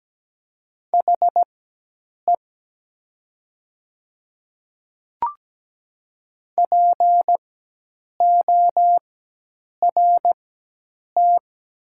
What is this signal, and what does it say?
Morse code, heard as a keyed tone, about 17 wpm, spaced as for 8 wpm, 700 Hz. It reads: HE PORT
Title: 4x Character Spacing - Top 300 Words - 17wpm